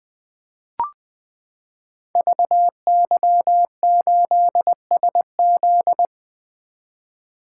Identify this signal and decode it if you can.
Morse code, heard as a keyed tone, about 20 wpm, 700 Hz. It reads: VY8SZ
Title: Code Groups - 5 Letter:Numbers - 20wpm